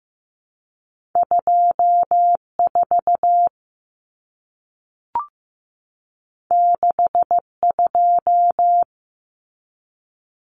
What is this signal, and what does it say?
Morse code, heard as a keyed tone, about 15 wpm, 700 Hz. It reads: 24 62
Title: Common Cut Numbers - 2-Digit 15wpm